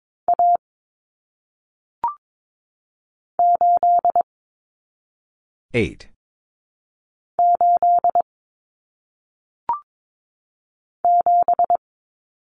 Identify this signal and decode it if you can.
Morse code, heard as a keyed tone, about 22 wpm, 700 Hz. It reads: A 8 8 7
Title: Common Cut Numbers - 1-Digit 22wpm